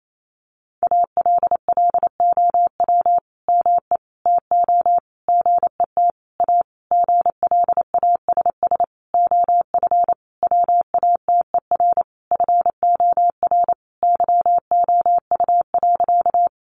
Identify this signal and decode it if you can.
Morse code, heard as a keyed tone, about 28 wpm, 700 Hz. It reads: ALLOW ME TO GET A GLASS OF WATER FOR YOU.
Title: Sentences from Top 1300 Words 28wpm